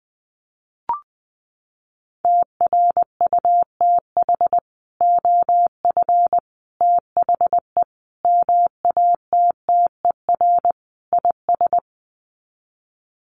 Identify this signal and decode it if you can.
Morse code, heard as a keyed tone, about 20 wpm, 700 Hz. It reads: TRUTH OF THE MATTER IS
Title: Sets of 5 Words Encore2 20wpm